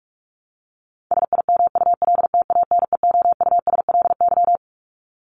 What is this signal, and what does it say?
Morse code, heard as a keyed tone, about 44 wpm, 700 Hz. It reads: SIMULTANEOUSLY
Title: LY Suffix - Top 100 Words with Base Words 45wpm